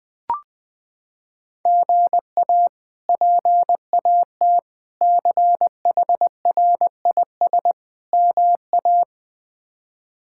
Morse code, heard as a keyed tone, about 20 wpm, 700 Hz. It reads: GA PAT CHRIS MA
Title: K1USN SST QSO Practice - 20wpm